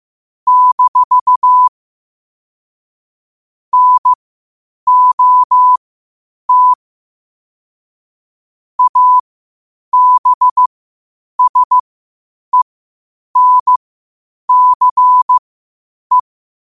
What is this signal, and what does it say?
Morse code, heard as a keyed tone, about 15 wpm, spaced as for 8 wpm, 1000 Hz. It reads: - NOT ABSENCE
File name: qotdfiles / 8wpm